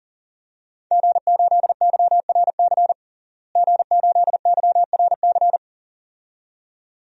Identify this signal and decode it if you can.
Morse code, heard as a keyed tone, about 40 wpm, 700 Hz. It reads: G8YRC G8YRC
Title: DXCC-100 Call Signs - 5 Characters - Repeated 2x - 40wpm